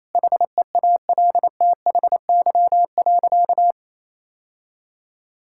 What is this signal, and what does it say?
Morse code, heard as a keyed tone, about 28 wpm, 700 Hz. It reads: HEALTHY.
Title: Sentences from Top 1000 Words - Review Part 5 - 28wpm